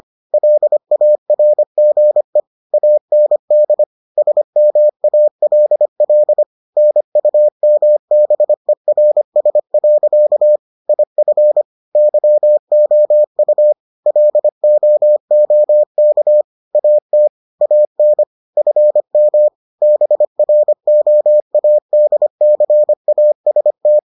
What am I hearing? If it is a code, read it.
Morse code, heard as a keyed tone, about 25 wpm, 600 Hz. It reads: LARGE AND SMALL NUMBERS. IF YOU LOOK AT AN FM BROADCAST